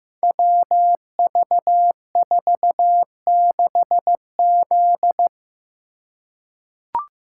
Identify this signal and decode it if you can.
Morse code, heard as a keyed tone, about 15 wpm, 700 Hz. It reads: WV46Z
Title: Code Groups - 5 Letter:Numbers - 15wpm